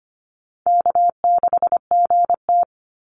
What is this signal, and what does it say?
Morse code, heard as a keyed tone, about 25 wpm, 700 Hz. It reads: K6GT
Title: Call Signs - 25wpm